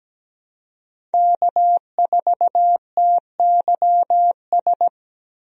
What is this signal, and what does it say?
Morse code, heard as a keyed tone, about 17 wpm, 700 Hz. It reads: K4TYS
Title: USA Call Signs - 5 Characters - Repeated 2x - 17wpm